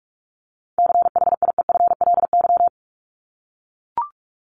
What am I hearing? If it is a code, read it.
Morse code, heard as a keyed tone, about 44 wpm, 700 Hz. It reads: CHIEFLY
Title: LY Suffix - Top 100 Words with Base Words 45wpm